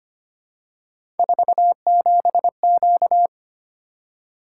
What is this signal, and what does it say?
Morse code, heard as a keyed tone, about 25 wpm, 700 Hz. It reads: 47Q
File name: Three Letter-Number Combinations - 25wpm